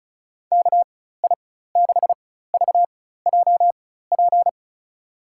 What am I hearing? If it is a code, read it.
Morse code, heard as a keyed tone, about 34 wpm, spaced as for 16 wpm, 700 Hz. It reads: KI6VJP
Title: USA Call Signs - 4x Character Spacing - 35wpm